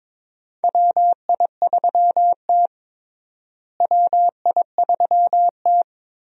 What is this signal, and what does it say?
Morse code, heard as a keyed tone, about 22 wpm, 700 Hz. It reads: WI3T WI3T
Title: USA Call Signs - 4 Characters - Repeated 3x - 22wpm